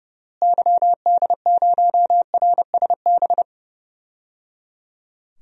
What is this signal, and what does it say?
Morse code, heard as a keyed tone, about 30 wpm, 700 Hz. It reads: YD0RSB
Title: DXCC-100 Call Signs - 6 Characters - Repeated 3x - 30wpm